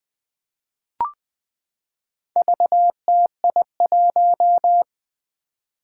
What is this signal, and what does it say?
Morse code, heard as a keyed tone, about 20 wpm, 700 Hz. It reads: VTI1
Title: Four Challenging Letter-Number Combinations 20wpm